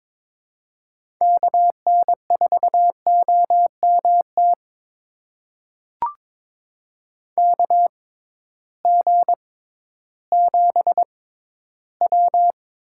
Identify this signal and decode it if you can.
Morse code, heard as a keyed tone, about 21 wpm, spaced as for 13 wpm, 700 Hz. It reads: KN4OMT KG7W
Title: USA Call Signs - 6 Characters - 6x Character Spacing - 22wpm